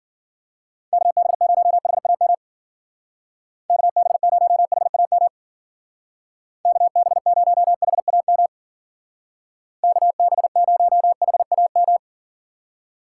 Morse code, heard as a keyed tone, about 50 wpm, 700 Hz. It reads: KB0HAM KB0HAM KB0HAM KB0HAM
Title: Speed Racing - USA 2x3 Call Signs 40wpm